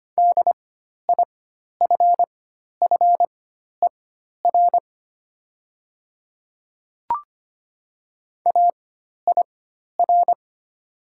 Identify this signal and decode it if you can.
Morse code, heard as a keyed tone, about 25 wpm, spaced as for 12 wpm, 700 Hz. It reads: DIFFER AIR
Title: 4x Character Spacing - Top 300 Words - 25wpm